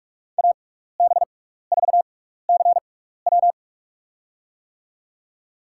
Morse code, heard as a keyed tone, about 45 wpm, spaced as for 15 wpm, 700 Hz. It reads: AB4CW
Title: USA Call Signs - 5 Characters - 6x Character Spacing - 45wpm